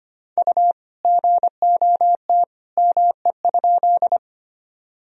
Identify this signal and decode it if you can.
Morse code, heard as a keyed tone, about 25 wpm, 700 Hz. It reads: U GOT ME?
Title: Sentences from Top 300 Words and Abbreviations 25wpm